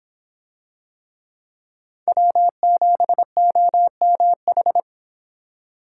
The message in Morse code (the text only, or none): W7OMH